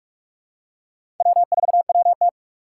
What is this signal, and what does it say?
Morse code, heard as a keyed tone, about 46 wpm, 700 Hz. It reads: W4WT
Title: USA Call Signs - 4 Characters - 6x Character Spacing - 45wpm